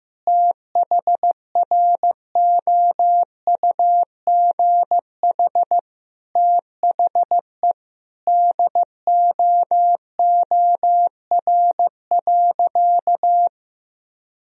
Morse code, heard as a keyed tone, about 16 wpm, 700 Hz. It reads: THROUGH THE DOOR.
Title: Sentences from Top 400 Words and Abbreviations 15wpm